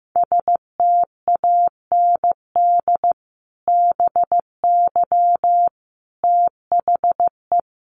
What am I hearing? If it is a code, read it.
Morse code, heard as a keyed tone, about 15 wpm, 700 Hz. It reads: STAND BY THE
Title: Sentences from Top 1200 Words 15wpm